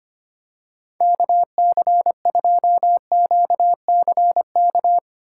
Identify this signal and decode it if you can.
Morse code, heard as a keyed tone, about 25 wpm, 700 Hz. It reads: KC2QCK
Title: USA Call Signs - 2x Character Spacing - 25wpm